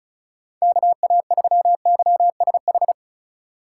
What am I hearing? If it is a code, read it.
Morse code, heard as a keyed tone, about 36 wpm, 700 Hz. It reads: KA3YSH
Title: USA Call Signs - 2x Character Spacing - 35wpm